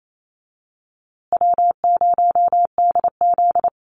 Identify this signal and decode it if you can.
Morse code, heard as a keyed tone, about 28 wpm, 700 Hz. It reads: W0DZ
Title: USA Call Signs Repeated 2x 28wpm